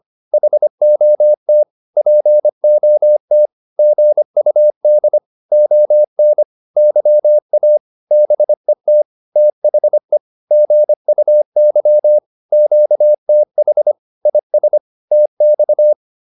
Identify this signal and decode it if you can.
Morse code, heard as a keyed tone, about 25 wpm, 600 Hz. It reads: HOT POT GUD ON YA BET THE GUY QTH IS TX